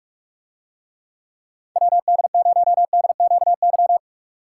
Morse code, heard as a keyed tone, about 46 wpm, 700 Hz. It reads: WD0DQY